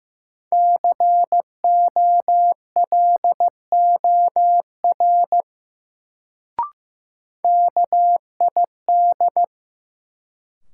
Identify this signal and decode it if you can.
Morse code, heard as a keyed tone, about 15 wpm, 700 Hz. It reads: COLOR KID
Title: Sentences from Top 1600 Words - New Word Drill Alt - 15wpm